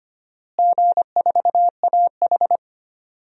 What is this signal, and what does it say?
Morse code, heard as a keyed tone, about 25 wpm, 700 Hz. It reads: G4AH
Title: DXCC-100 Call Signs - 4 Characters - Repeated 3x - 25wpm